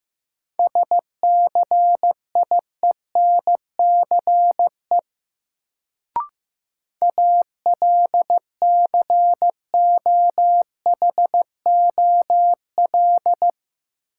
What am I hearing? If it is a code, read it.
Morse code, heard as a keyed tone, about 15 wpm, 700 Hz. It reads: SCIENCE ALCOHOL
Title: Sentences from Top 1400 Words - New Word Drill Alt - 15wpm